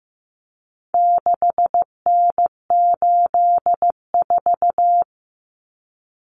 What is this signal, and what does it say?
Morse code, heard as a keyed tone, about 15 wpm, 700 Hz. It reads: 6N84
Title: Common Cut Numbers - 4-Digit 15wpm